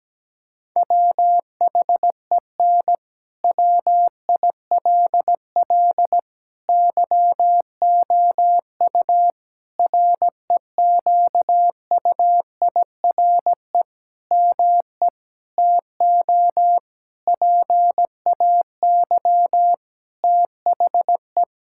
Morse code, heard as a keyed tone, about 17 wpm, 700 Hz. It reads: WHEN WILL YOU REQUIRE ME TO PAY THE
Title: Sentences from Top 1000 Words - Review Part 6 - 17wpm